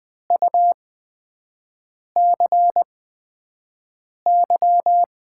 Morse code, heard as a keyed tone, about 20 wpm, spaced as for 5 wpm, 700 Hz. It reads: UCY
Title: USA Call Signs - 5 Characters - 8x Character Spacing - 20wpm